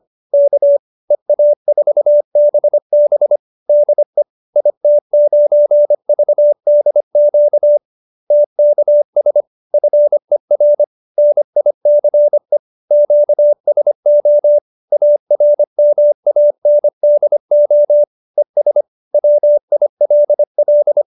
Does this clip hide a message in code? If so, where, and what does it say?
Yes: Morse code from 0.3 to 21.0 s, heard as a keyed tone, about 25 wpm, 590 Hz. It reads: K EA4BB DE IT9VDQ TKS FER NICE QSO ARMANDO ES WILL